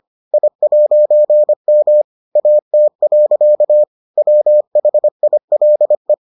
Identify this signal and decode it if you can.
Morse code, heard as a keyed tone, about 25 wpm, 600 Hz. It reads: I'M AT. WHILE